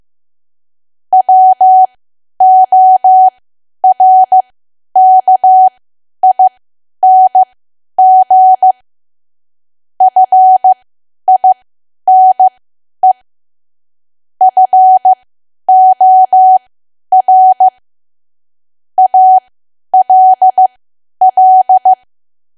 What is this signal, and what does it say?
Morse code, heard as a keyed tone, about 15 wpm, spaced as for 10 wpm, 750 Hz. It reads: WORKING FINE FOR ALL
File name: Archive / 10 WPM / 10WPM